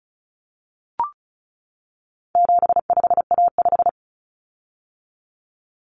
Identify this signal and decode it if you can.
Morse code, heard as a keyed tone, about 34 wpm, 700 Hz. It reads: 75A5
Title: Common Cut Numbers - 4-Digit 35wpm